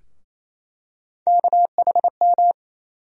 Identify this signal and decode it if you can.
Morse code, heard as a keyed tone, about 27 wpm, 700 Hz. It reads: KHM
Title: USA Call Sign Postfixes - 3 Letters 28wpm